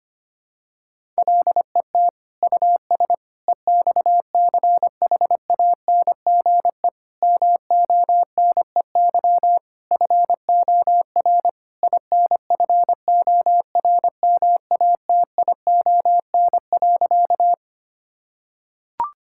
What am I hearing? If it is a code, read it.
Morse code, heard as a keyed tone, about 25 wpm, 700 Hz. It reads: LET US EXCHANGE MONEY FOR INFORMATION.
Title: Sentences from Top 1000 Words - Review Part 6 - 25wpm